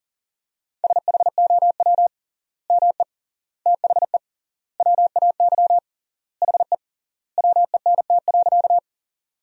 Morse code, heard as a keyed tone, about 40 wpm, 700 Hz. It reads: SHOW ME THE WAY HE WENT.
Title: 3x Word-Spacing - Sentences from Top 200 Words - 40wpm